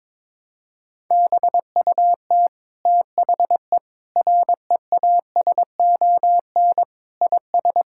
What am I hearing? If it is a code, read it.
Morse code, heard as a keyed tone, about 22 wpm, 700 Hz. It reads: BUT THE REASON IS
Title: Sentences from Top 1600 Words 22wpm